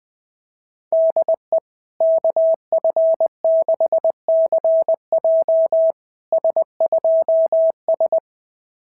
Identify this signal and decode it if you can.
Morse code, heard as a keyed tone, about 22 wpm, 650 Hz. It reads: DE KF6CJ S2S
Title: SOTA QSO Practice 20wpm